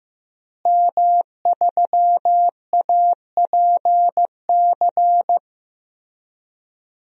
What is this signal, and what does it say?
Morse code, heard as a keyed tone, about 15 wpm, 700 Hz. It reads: M3APC
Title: DXCC-100 Call Signs - 5 Characters - Repeated 3x - 15wpm